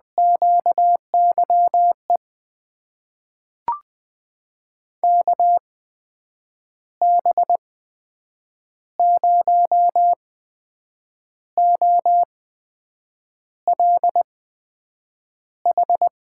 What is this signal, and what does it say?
Morse code, heard as a keyed tone, about 21 wpm, spaced as for 7 wpm, 700 Hz. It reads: QYE KB0OLH
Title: USA Call Signs - 6 Characters - 8x Character Spacing - 20wpm